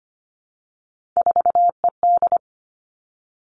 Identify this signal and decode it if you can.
Morse code, heard as a keyed tone, about 25 wpm, 700 Hz. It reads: 4ED
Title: Three Challenging Letter-Number Combinations 25wpm